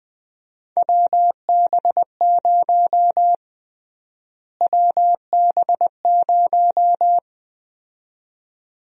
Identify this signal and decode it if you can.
Morse code, heard as a keyed tone, about 20 wpm, 700 Hz. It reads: WB0 WB0
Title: USA Call Sign Prefixes - Repeated 2x - 20wpm